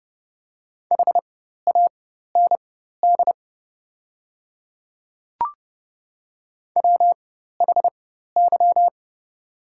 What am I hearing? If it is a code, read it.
Morse code, heard as a keyed tone, about 30 wpm, spaced as for 14 wpm, 700 Hz. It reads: HAND WHY
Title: 4x Character Spacing - Top 200 Words - 30wpm